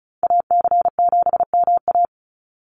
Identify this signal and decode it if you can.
Morse code, heard as a keyed tone, about 34 wpm, 700 Hz. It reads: AC7MA
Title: NA SOTA Activator Callsigns 35wpm